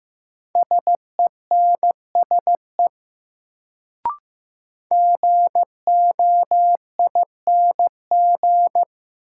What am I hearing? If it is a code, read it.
Morse code, heard as a keyed tone, about 15 wpm, 700 Hz. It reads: SENSE GOING
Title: Sets of 2 Words - New Word Drill - 15wpm